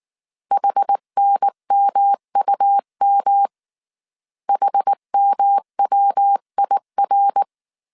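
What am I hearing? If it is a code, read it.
Morse code, heard as a keyed tone, about 19 wpm, 800 Hz. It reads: HNMUM HMWIR
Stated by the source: Morse code dictation at 18 wpm from the IW3SGT QRS Morse code training course, five-character groups of 300 characters.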